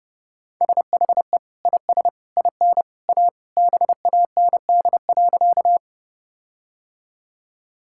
Morse code, heard as a keyed tone, about 32 wpm, 700 Hz. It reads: SHE IS IN A BAND.
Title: Sentences from Top 1600 Words 30wpm